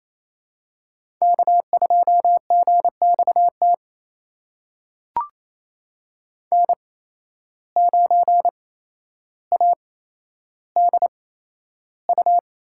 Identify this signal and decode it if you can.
Morse code, heard as a keyed tone, about 27 wpm, spaced as for 12 wpm, 700 Hz. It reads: K2GXT N9ADU
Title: USA Call Signs - 5 Characters - 8x Character Spacing - 28wpm